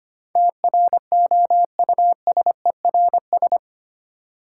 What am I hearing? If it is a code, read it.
Morse code, heard as a keyed tone, about 25 wpm, 700 Hz. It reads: TROUSERS
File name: 8 Letter Words - First 3 Letters Given - 25wpm